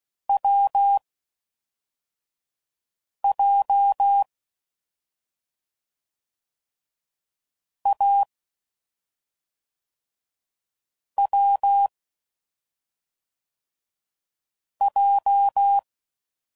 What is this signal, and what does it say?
Morse code, heard as a keyed tone, about 16 wpm, 800 Hz. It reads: W J A W J
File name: Combinations / Difficult Combos / AWJ